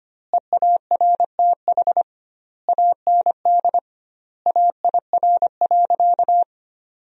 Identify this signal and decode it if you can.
Morse code, heard as a keyed tone, about 25 wpm, 700 Hz. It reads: EARTH AND AIR.
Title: 2x Word-Spacing - Sentences from Top 400 Words - 25wpm